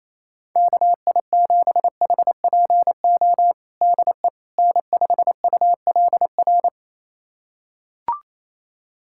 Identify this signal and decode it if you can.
Morse code, heard as a keyed tone, about 28 wpm, 700 Hz. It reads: KI7HPO DE N5ULR
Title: USA Callsign DE Callsign 28wpm